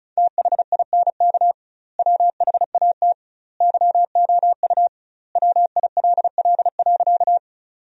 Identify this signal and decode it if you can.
Morse code, heard as a keyed tone, about 36 wpm, 700 Hz. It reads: THINK WHAT YOU WILL.
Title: 2x Word-Spacing - Sentences from Top 200 Words - 35wpm